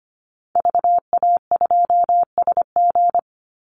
Morse code, heard as a keyed tone, about 25 wpm, 700 Hz. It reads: VA2SG
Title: Call Signs Repeated 2x - 25wpm